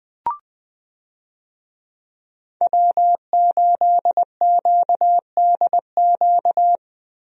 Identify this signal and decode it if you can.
Morse code, heard as a keyed tone, about 21 wpm, 700 Hz. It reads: W8QDQ
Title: USA Call Signs - 5 Characters - Repeated 2x - 20wpm